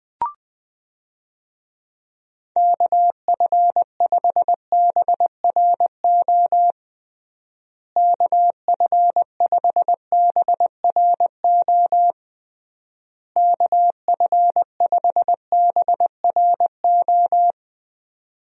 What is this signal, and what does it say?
Morse code, heard as a keyed tone, about 20 wpm, 700 Hz. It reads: KF5BRO KF5BRO KF5BRO
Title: USA Call Signs - 6 Characters - Repeated 3x - 20wpm